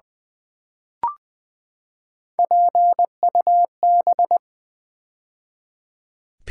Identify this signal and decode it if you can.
Morse code, heard as a keyed tone, about 20 wpm, 700 Hz. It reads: PUB
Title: Build A Word - 3 Letters At A Time - 9-Letter Words - 20wpm